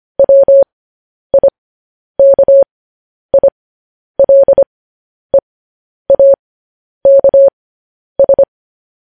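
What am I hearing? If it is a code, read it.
Morse code, heard as a keyed tone, about 25 wpm, spaced as for 10 wpm, 550 Hz. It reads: WIKILEAKS
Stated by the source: Morse code news headlines at 10 wpm.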